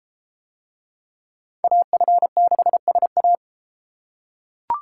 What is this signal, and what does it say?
Morse code, heard as a keyed tone, about 30 wpm, 700 Hz. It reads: AF6SA